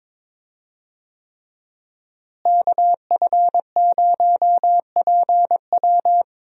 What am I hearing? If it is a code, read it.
Morse code, heard as a keyed tone, about 21 wpm, 700 Hz. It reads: KF0PW
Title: USA Call Signs - 5 Characters - Repeated 3x - 22wpm